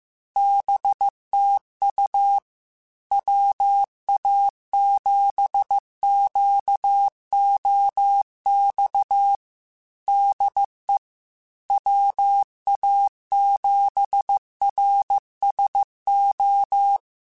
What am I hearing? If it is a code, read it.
Morse code, heard as a keyed tone, about 15 wpm, 800 Hz. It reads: BTU WA7QOX DE WA7RSO
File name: QSOs / QSOs-mp3s / WA7QOX 1@13